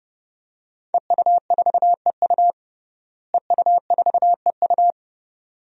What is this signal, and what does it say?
Morse code, heard as a keyed tone, about 31 wpm, 700 Hz. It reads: EU4EU EU4EU
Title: DXCC-100 Call Signs - 5 Characters - Repeated 2x - 30wpm